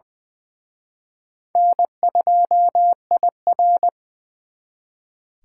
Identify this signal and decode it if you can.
Morse code, heard as a keyed tone, about 24 wpm, 700 Hz. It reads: N2IR
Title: Speed Racing - USA Call Signs 20wpm